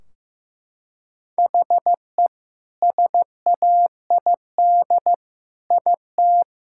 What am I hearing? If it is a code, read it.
Morse code, heard as a keyed tone, about 15 wpm, 700 Hz. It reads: HE SAID IT
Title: Sentences from Top 1600 Words 15wpm